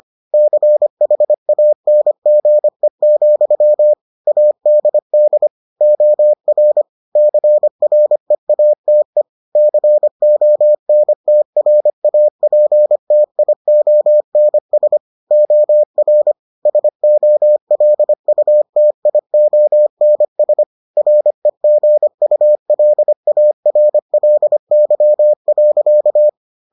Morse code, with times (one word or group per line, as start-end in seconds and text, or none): 0.3-3.9 CHANGE,
4.3-5.5 ADD
5.8-6.8 OR
7.2-9.2 CREATE
9.6-15.0 CONTRAPTIONS
15.3-16.3 OR
16.7-20.6 SOLUTIONS
21.0-26.3 REGULARLY.